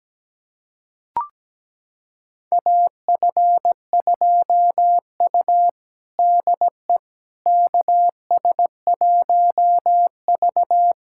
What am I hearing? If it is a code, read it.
Morse code, heard as a keyed tone, about 17 wpm, 700 Hz. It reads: AF2U DE KS1V
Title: USA Callsign DE Callsign - 4 Characters - 17wpm